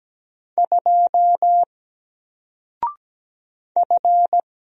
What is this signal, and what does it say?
Morse code, heard as a keyed tone, about 17 wpm, 700 Hz. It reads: 2 F
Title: Single Letter-Number 17wpm